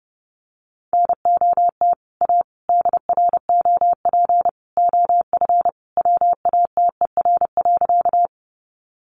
Morse code, heard as a keyed tone, about 29 wpm, 700 Hz. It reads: NOT A DROP OF WATER.
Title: Sentences from Top 1400 Words 30wpm